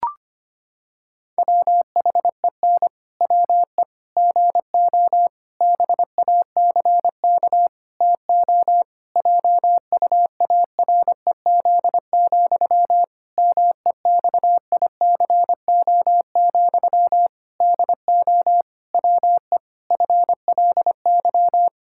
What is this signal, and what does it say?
Morse code, heard as a keyed tone, about 25 wpm, 700 Hz. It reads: WHEN WE GO BACK TO JUAREZ, MEXICO, DO WE FLY
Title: Pangram Challenge v2 - 25wpm